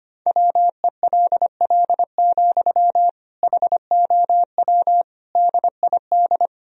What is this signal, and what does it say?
Morse code, heard as a keyed tone, about 25 wpm, 700 Hz. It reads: WELL, HOW DID
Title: Sentences from Top 1000 Words - Review Part 2 - 25wpm